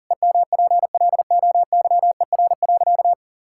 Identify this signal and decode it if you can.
Morse code, heard as a keyed tone, about 37 wpm, 700 Hz. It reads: EMPLOYER.
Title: Sentences from Top 1600 Words 40wpm